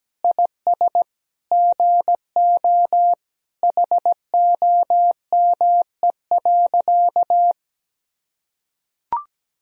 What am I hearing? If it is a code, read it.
Morse code, heard as a keyed tone, about 17 wpm, 700 Hz. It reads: IS GO HOME.
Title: Sentences from Top 200 Words and Abbreviations 17wpm